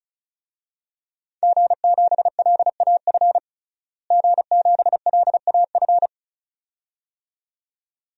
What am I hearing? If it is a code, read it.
Morse code, heard as a keyed tone, about 34 wpm, 700 Hz. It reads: G7LAF G7LAF
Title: DXCC-100 Call Signs - 5 Characters - Repeated 2x - 35wpm